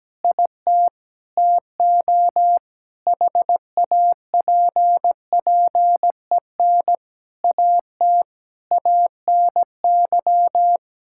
Morse code, heard as a keyed tone, about 17 wpm, 700 Hz. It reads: IT TO HAPPEN AT ANY
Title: Sentences from Top 1000 Words - Review Part 6 - 17wpm